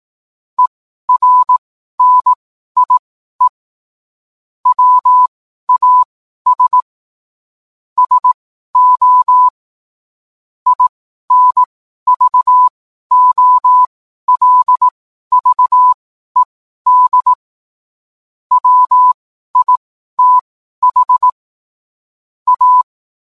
Morse code, heard as a keyed tone, about 18 wpm, spaced as for 12 wpm, 1000 Hz. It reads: ERNIE WAS SO INVOLVED WITH A